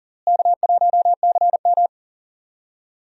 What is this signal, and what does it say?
Morse code, heard as a keyed tone, about 39 wpm, 700 Hz. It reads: K1CM
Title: USA Call Signs - 4 Characters - Repeated 2x - 40wpm